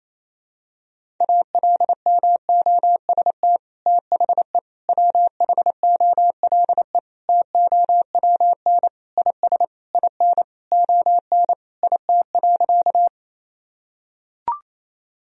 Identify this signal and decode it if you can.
Morse code, heard as a keyed tone, about 28 wpm, 700 Hz. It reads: ALMOST THE WHOLE TOWN IS IN ON IT.
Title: Sentences from Top 1000 Words - Review Part 4 - 28wpm